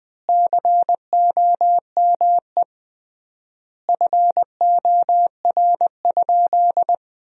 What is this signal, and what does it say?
Morse code, heard as a keyed tone, about 20 wpm, 700 Hz. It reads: COME FOR?
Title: 3x Word-Spacing - Sentences from Top 100 Words - 20wpm